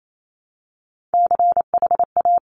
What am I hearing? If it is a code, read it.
Morse code, heard as a keyed tone, about 28 wpm, 700 Hz. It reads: CHA
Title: Top Three Letter Combinations in Words 28wpm